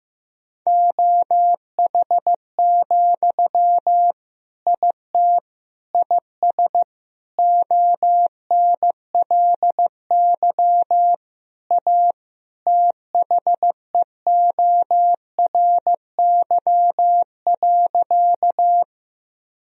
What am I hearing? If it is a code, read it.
Morse code, heard as a keyed tone, about 15 wpm, 700 Hz. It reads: OH, IT IS ONLY A THEORY.
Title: Sentences from Top 1000 Words - Review Part 3 - 15wpm